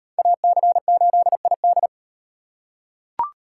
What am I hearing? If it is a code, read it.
Morse code, heard as a keyed tone, about 37 wpm, 700 Hz. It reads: AC8ID